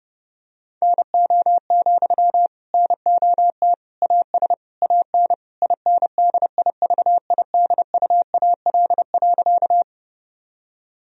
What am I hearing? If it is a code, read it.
Morse code, heard as a keyed tone, about 32 wpm, 700 Hz. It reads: NO, NOT AS AN INDIVIDUAL.
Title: Sentences from Top 1000 Words - Review Part 4 - 30wpm